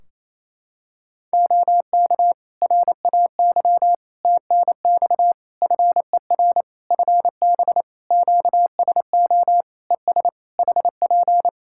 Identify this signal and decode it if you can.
Morse code, heard as a keyed tone, about 28 wpm, 700 Hz. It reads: OK RAY TNX FER FB QSO ES HP
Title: Practice QSO - Exchange 4 - Ending - 28wpm